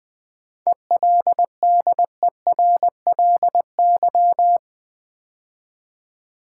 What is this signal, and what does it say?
Morse code, heard as a keyed tone, about 20 wpm, 700 Hz. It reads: ELDERLY
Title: 7 Letter Words - First 3 Letters Given - 20wpm